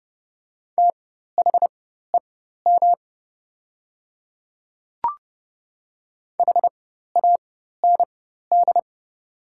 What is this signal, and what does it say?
Morse code, heard as a keyed tone, about 30 wpm, spaced as for 14 wpm, 690 Hz. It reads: THEM HAND
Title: 4x Character Spacing - Top 500 Words - 30wpm